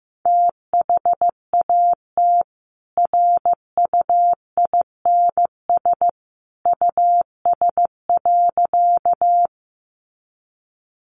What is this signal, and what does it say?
Morse code, heard as a keyed tone, about 15 wpm, 700 Hz. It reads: THAT RUINS US.
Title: A Sound Of Thunder 15wpm